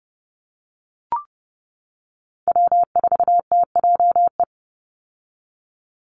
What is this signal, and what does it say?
Morse code, heard as a keyed tone, about 30 wpm, 700 Hz. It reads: W4TJE
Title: Call Signs - 30wpm